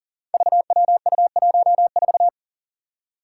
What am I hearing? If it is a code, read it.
Morse code, heard as a keyed tone, about 40 wpm, 700 Hz. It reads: VWU14
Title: Five Challenging Letter-Number Combinations 40wpm